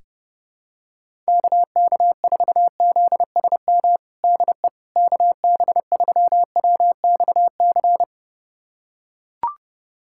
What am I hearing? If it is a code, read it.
Morse code, heard as a keyed tone, about 30 wpm, 700 Hz. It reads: KK4ZSM DE KB3WXC
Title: USA Callsign DE Callsign - 6 Characters - 30wpm